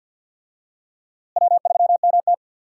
Morse code, heard as a keyed tone, about 49 wpm, 700 Hz. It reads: W3MT
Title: USA Call Signs - 4 Characters - Repeated 3x - 50wpm